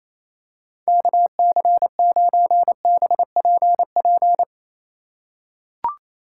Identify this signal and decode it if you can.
Morse code, heard as a keyed tone, about 28 wpm, 700 Hz. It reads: KC9BPP
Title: USA Call Signs - 6 Characters - 4x Character Spacing - 28wpm